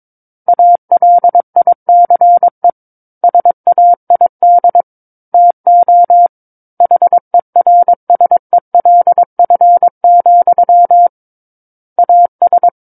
Morse code, heard as a keyed tone, about 22 wpm, 700 Hz. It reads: ALICE SAID TO HERSELF, AS
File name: 06-Chapter-06 22 WPM